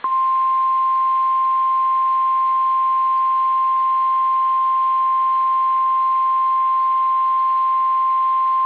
Emergency Alert System (SAME) tones (the 1050 Hz attention tone). It is heard from 1.1 to 7.7 s.
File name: eas / KSDZKDJL / KSDZKDJL